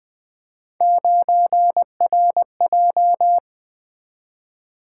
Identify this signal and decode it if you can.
Morse code, heard as a keyed tone, about 24 wpm, 700 Hz. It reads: 9RJ
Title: Speed-Racing - Three Letter-Number Combinations - 20wpm